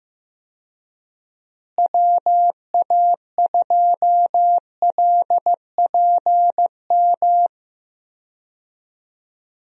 Morse code, heard as a keyed tone, about 15 wpm, 700 Hz. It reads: WA2LPM
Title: USA Call Signs - 6 Characters - Repeated 3x - 15wpm